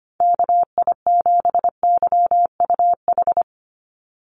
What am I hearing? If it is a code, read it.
Morse code, heard as a keyed tone, about 25 wpm, 700 Hz. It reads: KI7YUH
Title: USA Call Signs 25wpm